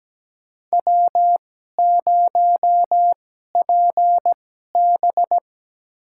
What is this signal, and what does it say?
Morse code, heard as a keyed tone, about 17 wpm, spaced as for 12 wpm, 700 Hz. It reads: W0PB
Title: USA Call Signs - 4 Characters - 2x Character Spacing - 17wpm